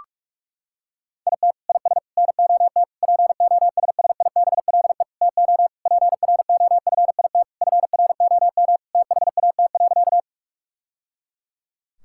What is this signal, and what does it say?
Morse code, heard as a keyed tone, about 41 wpm, 700 Hz. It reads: IT IS NOT POSSIBLE TO PROFIT FROM THAT.
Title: Sentences from Top 1000 Words - Review Part 5 - 45wpm